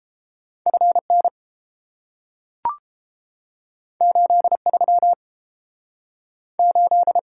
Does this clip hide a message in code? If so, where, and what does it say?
Yes: Morse code from 0.7 to 7.2 s, heard as a keyed tone, about 30 wpm, 700 Hz. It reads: F N 8 3 8